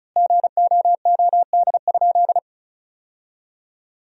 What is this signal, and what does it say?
Morse code, heard as a keyed tone, about 36 wpm, 700 Hz. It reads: GOOD?
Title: 4x Word-Spacing - Sentences from Top 200 Words - 35wpm